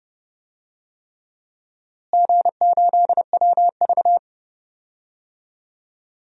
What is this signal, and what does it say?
Morse code, heard as a keyed tone, about 30 wpm, 700 Hz. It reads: G8WV
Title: DXCC-100 Call Signs - 4 Characters - Repeated 3x - 30wpm